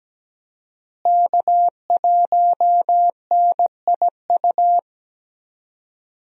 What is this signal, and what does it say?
Morse code, heard as a keyed tone, about 17 wpm, 700 Hz. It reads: K1NIU
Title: USA Call Signs - 5 Characters - Repeated 2x - 17wpm